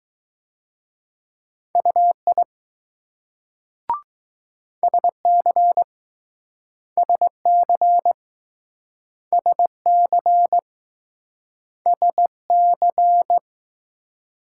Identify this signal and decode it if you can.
Morse code, heard as a keyed tone, about 20 wpm, 700 Hz. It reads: UI SC SC SC SC